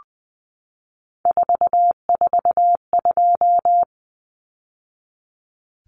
Morse code, heard as a keyed tone, about 19 wpm, 700 Hz. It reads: 442
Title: Numbers - 3-Digit 20wpm